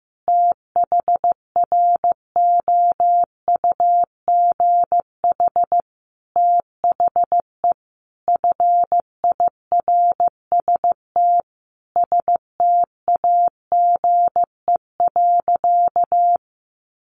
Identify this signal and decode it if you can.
Morse code, heard as a keyed tone, about 14 wpm, 700 Hz. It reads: THROUGH THE FIRST STAGE.
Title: Sentences from Top 1200 Words 15wpm